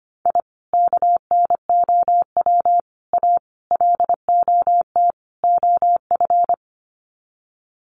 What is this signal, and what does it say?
Morse code, heard as a keyed tone, about 25 wpm, 700 Hz. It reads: I KNOW A LOT OF
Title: Sets of 5 Words Encore 25wpm